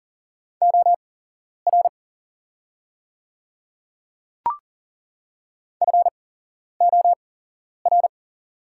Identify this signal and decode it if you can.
Morse code, heard as a keyed tone, about 40 wpm, spaced as for 11 wpm, 700 Hz. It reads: OR FOR
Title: 8x Character Spacing - Top 100 Words - 40wpm